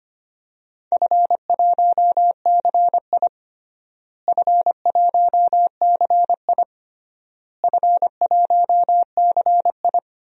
Morse code, heard as a keyed tone, about 25 wpm, 700 Hz. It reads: F1CI F1CI F1CI
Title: DXCC-100 Call Signs - 4 Characters - Repeated 3x - 25wpm